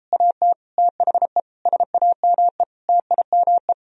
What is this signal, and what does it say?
Morse code, heard as a keyed tone, about 33 wpm, 700 Hz. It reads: AT THE SAME TIME